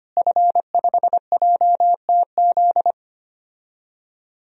Morse code, heard as a keyed tone, about 25 wpm, 700 Hz. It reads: F5JTZ
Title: DXCC-100 Call Signs - 5 Characters - Repeated 2x - 25wpm